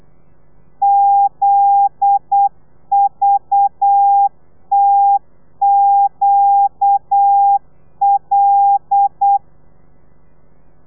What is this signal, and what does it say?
Morse code, heard as a keyed tone, about 8 wpm, 800 Hz. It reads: ZVTQL
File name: telegrafia / i2tfj / Lez9 35